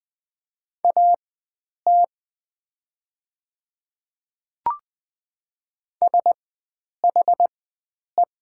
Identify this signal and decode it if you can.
Morse code, heard as a keyed tone, about 20 wpm, spaced as for 9 wpm, 700 Hz. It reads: AT SHE
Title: 4x Character Spacing - Top 100 Words - 20wpm